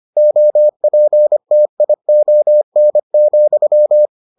Morse code, heard as a keyed tone, about 25 wpm, 600 Hz. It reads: OPTION,